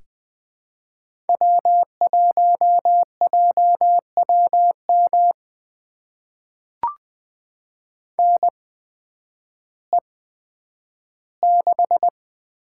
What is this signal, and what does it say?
Morse code, heard as a keyed tone, about 20 wpm, spaced as for 11 wpm, 700 Hz. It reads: W1JWM NE6
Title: USA Call Signs - 8x Character Spacing - 20wpm